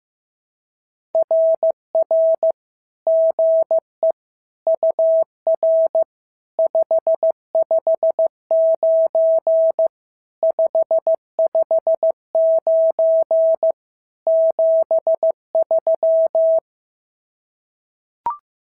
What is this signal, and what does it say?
Morse code, heard as a keyed tone, about 15 wpm, 650 Hz. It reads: RR GE UR 559 559 73
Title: SOTA QSO Practice 15wpm